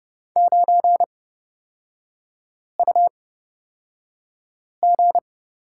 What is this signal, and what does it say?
Morse code, heard as a keyed tone, about 31 wpm, spaced as for 5 wpm, 700 Hz. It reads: 9UG
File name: Lesson 36e - BT - Three Characters 30wpm